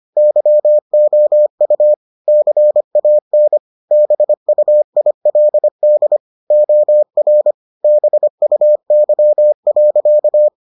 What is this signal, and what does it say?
Morse code, heard as a keyed tone, about 25 wpm, 600 Hz. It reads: YOU CAN BUILD OR BUY.